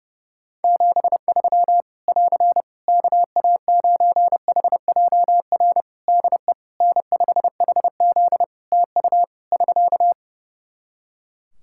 Morse code, heard as a keyed tone, about 30 wpm, 700 Hz. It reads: 73 <AR> KA9HJR DE N5HZ TU <SK>
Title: Practice QSO - Exchange 4 - Ending - 30wpm